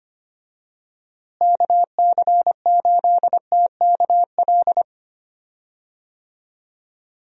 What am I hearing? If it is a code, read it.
Morse code, heard as a keyed tone, about 25 wpm, 700 Hz. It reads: KC8TKL
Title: USA Call Signs - 6 Characters - Repeated 3x - 25wpm